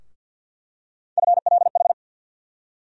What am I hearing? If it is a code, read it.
Morse code, heard as a keyed tone, about 49 wpm, 700 Hz. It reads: FLH
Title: USA Call Sign Postfixes - 3 Letters 50wpm